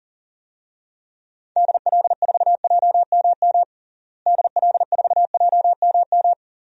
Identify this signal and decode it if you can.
Morse code, heard as a keyed tone, about 40 wpm, 700 Hz. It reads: DL4JMM DL4JMM
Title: DXCC-100 Call Signs - 6 Characters - Repeated 2x - 40wpm